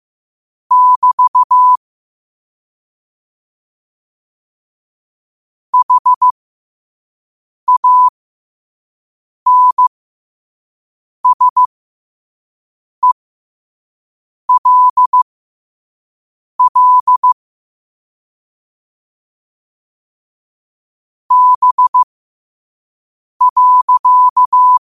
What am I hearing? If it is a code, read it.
Morse code, heard as a keyed tone, about 15 wpm, spaced as for 5 wpm, 1000 Hz. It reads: <BT> HANSELL B.
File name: qotdfiles / 5wpm